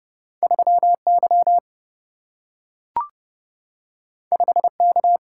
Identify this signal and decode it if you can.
Morse code, heard as a keyed tone, about 30 wpm, 700 Hz. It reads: 3 Y 5 K
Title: Single-Letter Single-Number 30wpm